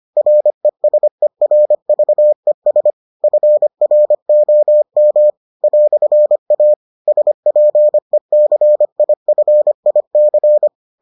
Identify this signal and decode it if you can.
Morse code, heard as a keyed tone, about 25 wpm, 600 Hz. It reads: RESERVES FROM "A SPECIFIC